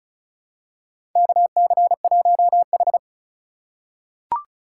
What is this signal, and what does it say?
Morse code, heard as a keyed tone, about 36 wpm, 700 Hz. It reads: KC1H
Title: USA Call Signs - 3x Character Spacing - 35wpm